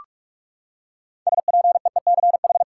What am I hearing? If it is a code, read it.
Morse code, heard as a keyed tone, about 46 wpm, 700 Hz. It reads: SPEECH
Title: Six Letter Words - 45wpm